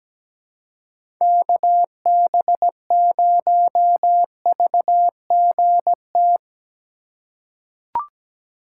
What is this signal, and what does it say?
Morse code, heard as a keyed tone, about 17 wpm, 700 Hz. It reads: KB0VGT
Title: USA Call Signs - 6 Characters - 3x Character Spacing - 17wpm